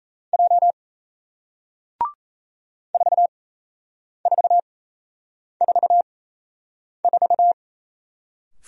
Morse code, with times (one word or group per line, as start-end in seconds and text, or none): 0.3-0.7 J
2.9-7.5 4444